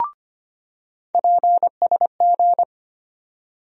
Morse code, heard as a keyed tone, about 25 wpm, 700 Hz. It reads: PSG
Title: USA Call Sign Postfixes - 3 Letters 25wpm